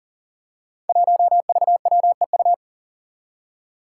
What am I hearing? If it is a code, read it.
Morse code, heard as a keyed tone, about 40 wpm, 700 Hz. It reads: 1VWEU
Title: Five Challenging Letter-Number Combinations 40wpm